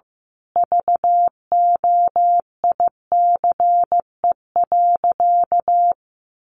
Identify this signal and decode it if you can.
Morse code, heard as a keyed tone, about 15 wpm, 700 Hz. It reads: VOICE.
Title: Sentences from Top 1100 Words 15wpm